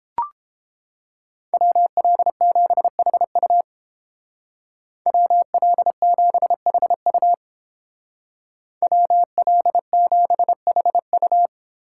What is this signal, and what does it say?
Morse code, heard as a keyed tone, about 29 wpm, 700 Hz. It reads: WL7HU WL7HU WL7HU